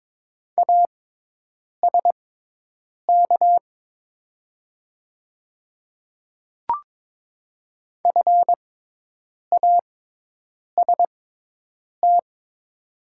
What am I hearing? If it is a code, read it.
Morse code, heard as a keyed tone, about 22 wpm, spaced as for 8 wpm, 700 Hz. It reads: ASK FAST
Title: 6x Character Spacing - Top 500 Words - 22wpm